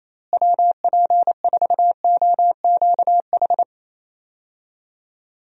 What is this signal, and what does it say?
Morse code, heard as a keyed tone, about 28 wpm, 700 Hz. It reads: WP4OQH
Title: USA Call Signs - 6 Characters - Repeated 2x - 28wpm